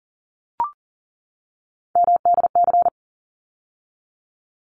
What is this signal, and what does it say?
Morse code, heard as a keyed tone, about 40 wpm, 700 Hz. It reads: MDC
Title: ARRL Sections 40wpm